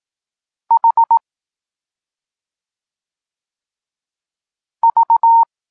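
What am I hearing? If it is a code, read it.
Morse code, heard as a keyed tone, about 18 wpm, 920 Hz. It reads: H V
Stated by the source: Polish Morse lesson teaching the letter C, from SP9MOA Niepołomicki Klub Krótkofalowców.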